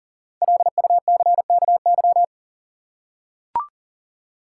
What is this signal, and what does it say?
Morse code, heard as a keyed tone, about 38 wpm, 700 Hz. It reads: LUCKY
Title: Five Letter Words - 40wpm